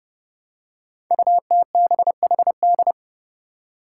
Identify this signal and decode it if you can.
Morse code, heard as a keyed tone, about 30 wpm, 700 Hz. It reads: UTBHD
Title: Five Challenging Letter-Number Combinations 30wpm